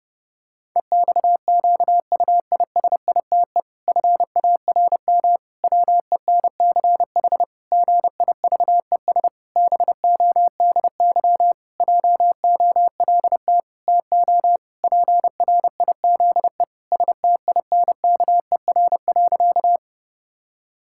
Morse code, heard as a keyed tone, about 32 wpm, 700 Hz. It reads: EXQUISITE FARM WENCH GIVES BODY JOLT TO PRIZE STINKER.
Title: Pangram Challenge v2 - 30wpm